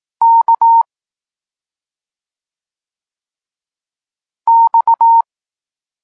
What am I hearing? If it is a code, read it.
Morse code, heard as a keyed tone, about 18 wpm, 920 Hz. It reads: K X
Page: Tekst 32 – Doskonalenie odbioru poznanych liter